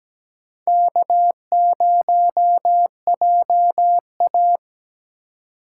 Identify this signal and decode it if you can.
Morse code, heard as a keyed tone, about 17 wpm, 700 Hz. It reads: K0JA
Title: USA Call Signs - 4x Character Spacing - 17wpm